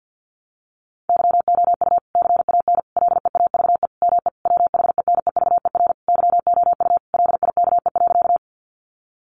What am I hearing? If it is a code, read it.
Morse code, heard as a keyed tone, about 50 wpm, 700 Hz. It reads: YOU CAN LEAVE ME WHENEVER YOU LIKE.
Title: Sentences from Top 1200 Words 50wpm